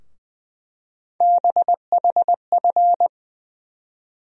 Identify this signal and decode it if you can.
Morse code, heard as a keyed tone, about 20 wpm, 700 Hz. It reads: BHF
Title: USA Call Sign Postfixes - 3 Letters 20wpm